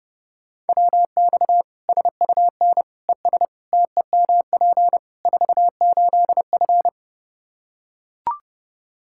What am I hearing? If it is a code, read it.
Morse code, heard as a keyed tone, about 31 wpm, 700 Hz. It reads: WX SUN ES TEMP 48F
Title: K1USN SST QSO Practice 30wpm